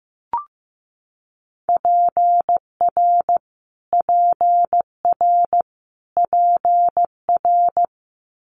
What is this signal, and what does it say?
Morse code, heard as a keyed tone, about 15 wpm, 700 Hz. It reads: PR PR PR
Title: Multi-letter Phonemes Repeated 3x 15wpm